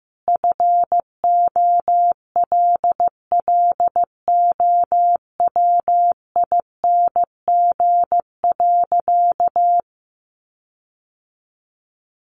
Morse code, heard as a keyed tone, about 15 wpm, 700 Hz. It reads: FOLLOWING.
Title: Sentences from Top 1300 Words 15wpm